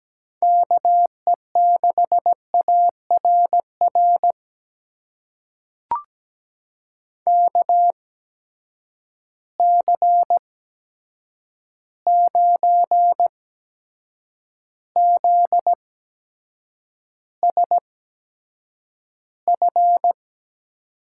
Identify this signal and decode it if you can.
Morse code, heard as a keyed tone, about 17 wpm, spaced as for 7 wpm, 700 Hz. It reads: KE6ARR KC9ZSF
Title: USA Call Signs - 6 Characters - 8x Character Spacing - 17wpm